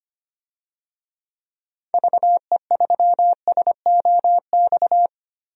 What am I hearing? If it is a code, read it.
Morse code, heard as a keyed tone, about 25 wpm, 700 Hz. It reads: VE3SOX
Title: DXCC-100 Call Signs - 6 Characters - Repeated 3x - 25wpm